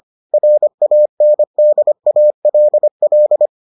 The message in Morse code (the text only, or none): RANDALL